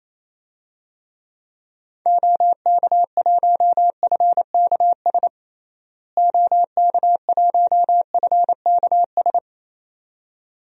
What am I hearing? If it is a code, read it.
Morse code, heard as a keyed tone, about 28 wpm, 700 Hz. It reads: OK1FKS OK1FKS
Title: DXCC-100 Call Signs - 6 Characters - Repeated 2x - 28wpm